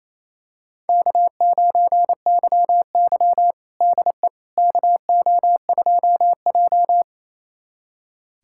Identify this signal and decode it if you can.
Morse code, heard as a keyed tone, about 28 wpm, 700 Hz. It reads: K9YY DE KO2J
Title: USA Callsign DE Callsign - 4 Characters - 28wpm